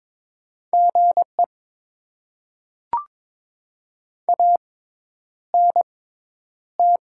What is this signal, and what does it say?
Morse code, heard as a keyed tone, about 22 wpm, spaced as for 10 wpm, 690 Hz. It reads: GE ANT
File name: Top 20 QSO Elements - 6x Character Spacing - 22wpm